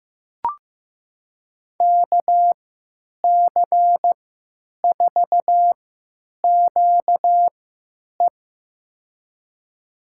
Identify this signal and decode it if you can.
Morse code, heard as a keyed tone, about 15 wpm, spaced as for 9 wpm, 700 Hz. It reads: KC4QE
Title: USA Call Signs - 5 Characters - 3x Character Spacing - 15wpm